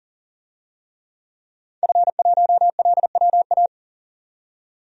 Morse code, heard as a keyed tone, about 40 wpm, 700 Hz. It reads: F1LWA
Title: DXCC-100 Call Signs - 5 Characters - 40wpm